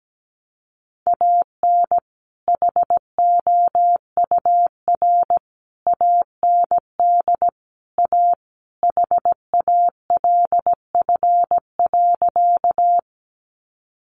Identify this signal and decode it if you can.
Morse code, heard as a keyed tone, about 17 wpm, 700 Hz. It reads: AN HOUR AND A HALF.
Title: Sentences from Top 400 Words 17wpm